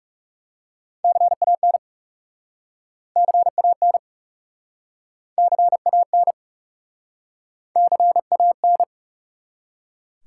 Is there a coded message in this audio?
Yes: Morse code from 1.0 to 8.8 s, heard as a keyed tone, about 38 wpm, 700 Hz. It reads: CAN CAN CAN CAN